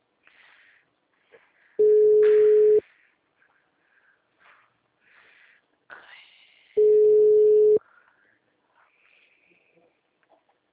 Ringback tone (Europe, 425 Hz).